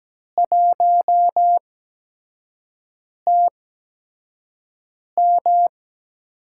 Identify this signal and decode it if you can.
Morse code, heard as a keyed tone, about 17 wpm, spaced as for 5 wpm, 700 Hz. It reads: 1TM
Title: USA Call Signs - 4 Characters - 8x Character Spacing - 17wpm